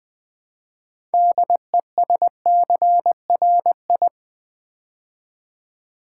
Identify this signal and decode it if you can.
Morse code, heard as a keyed tone, about 20 wpm, 700 Hz. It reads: DESCRI
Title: Build A Word - 3 Letters At A Time - 8-Letter Words - 20wpm